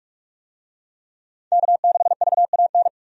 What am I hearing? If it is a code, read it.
Morse code, heard as a keyed tone, about 46 wpm, 700 Hz. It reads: K6VAN
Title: USA Call Signs - 5 Characters - 45wpm